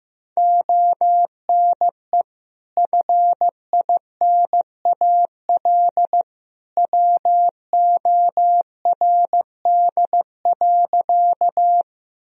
Morse code, heard as a keyed tone, about 14 wpm, 700 Hz. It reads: ONE FINAL WORD.
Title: Sentences from Top 1000 Words - Review Part 2 - 15wpm